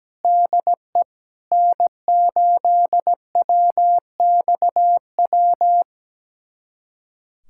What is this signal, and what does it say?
Morse code, heard as a keyed tone, about 17 wpm, 700 Hz. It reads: DE N8WXW
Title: USA Callsign DE Callsign - 5 Characters - 17wpm